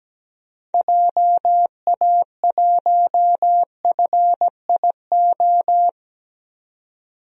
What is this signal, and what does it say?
Morse code, heard as a keyed tone, about 17 wpm, 700 Hz. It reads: JA1FIO
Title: DXCC-100 Call Signs - 6 Characters - 17wpm